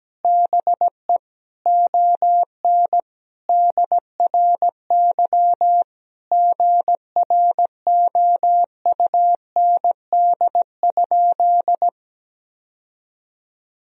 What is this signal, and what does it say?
Morse code, heard as a keyed tone, about 17 wpm, 700 Hz. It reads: BE ON DRY GROUND?
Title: Sentences from Top 1000 Words - Review Part 3 - 17wpm